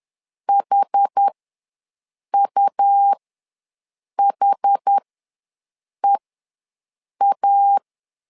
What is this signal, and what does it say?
Morse code, heard as a keyed tone, about 10 wpm, spaced as for 6 wpm, 800 Hz. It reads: HUHEA
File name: LEZIONI / 2 UHDN / 2 UH ws 10wpm 2029